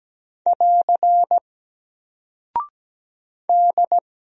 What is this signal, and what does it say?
Morse code, heard as a keyed tone, about 17 wpm, 700 Hz. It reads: <AR> D
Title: Single Character (Letter:Number:Punctuation:Prosign) Alt 17wpm